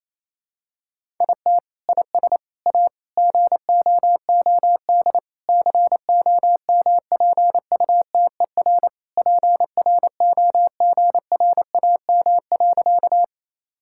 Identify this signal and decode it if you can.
Morse code, heard as a keyed tone, about 28 wpm, 700 Hz. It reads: IT IS A GOOD COMPUTER PROGRAM.
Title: Sentences from Top 1000 Words - Review Part 5 - 28wpm